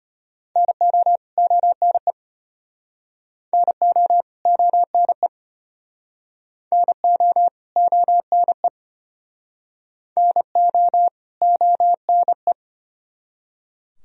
Morse code, heard as a keyed tone, about 33 wpm, 700 Hz. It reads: NO ONE NO ONE NO ONE NO ONE